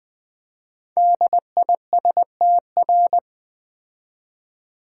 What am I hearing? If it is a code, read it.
Morse code, heard as a keyed tone, about 20 wpm, 700 Hz. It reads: DISTR
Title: Build A Word - 5 Letters At A Time - 12-Letter Words - 20wpm